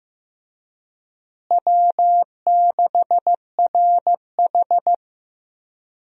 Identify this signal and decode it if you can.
Morse code, heard as a keyed tone, about 16 wpm, 700 Hz. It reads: W6RH
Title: USA Call Signs - 4 Characters - Repeated 3x - 15wpm